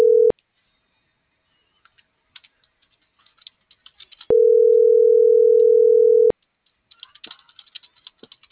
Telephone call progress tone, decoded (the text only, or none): ringback tone